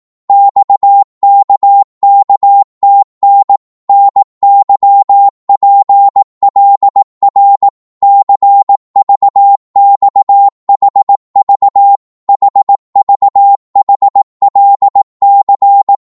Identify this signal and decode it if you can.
Morse code, heard as a keyed tone, about 18 wpm, 800 Hz. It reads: XKKTN NYPLR CVXHV HVHLC
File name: Morse / ltr-18wpm